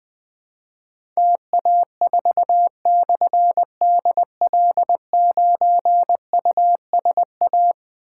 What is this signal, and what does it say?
Morse code, heard as a keyed tone, about 20 wpm, 700 Hz. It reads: TA4/DL9USA
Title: Difficult Call Signs 20wpm